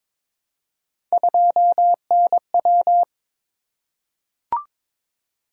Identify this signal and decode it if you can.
Morse code, heard as a keyed tone, about 21 wpm, 700 Hz. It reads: 2NW
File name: practice / 22wpm / Three Letter-Number Combinations - 22wpm